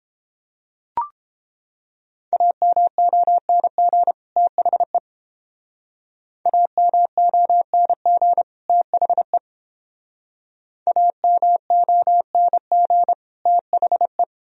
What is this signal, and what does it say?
Morse code, heard as a keyed tone, about 31 wpm, 700 Hz. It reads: AMONG THE AMONG THE AMONG THE